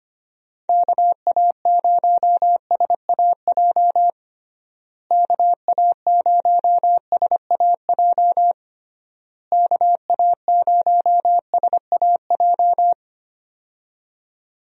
Morse code, heard as a keyed tone, about 25 wpm, 700 Hz. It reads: KA0SAJ KA0SAJ KA0SAJ
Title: USA Call Signs - 6 Characters - Repeated 3x - 25wpm